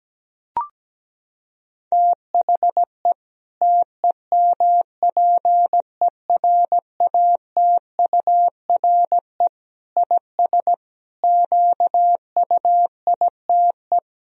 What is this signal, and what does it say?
Morse code, heard as a keyed tone, about 17 wpm, 700 Hz. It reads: THE TEMPERATURE IS QUITE
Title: Sentences from Top 1000 Words - Review Part 4 - 17wpm